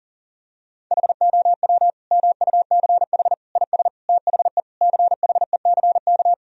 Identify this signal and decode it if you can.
Morse code, heard as a keyed tone, about 40 wpm, 700 Hz. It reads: HOW MUCH IS THE CHECK
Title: Sentences from Top 1000 Words - Review Part 3 - 40wpm